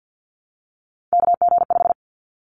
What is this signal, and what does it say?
Morse code, heard as a keyed tone, about 50 wpm, 700 Hz. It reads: KG5
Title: Callsign Prefixes for Top 25 DXCC - 50wpm